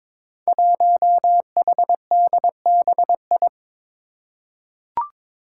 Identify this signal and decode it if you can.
Morse code, heard as a keyed tone, about 22 wpm, 700 Hz. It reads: 1HDBI
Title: Five Challenging Letter-Number Combinations 22wpm